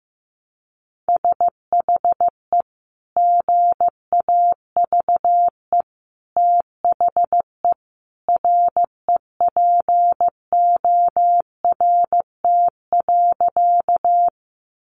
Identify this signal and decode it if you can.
Morse code, heard as a keyed tone, about 15 wpm, 700 Hz. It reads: SHE GAVE THE REPORT.
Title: Sentences from Top 900 Words 15wpm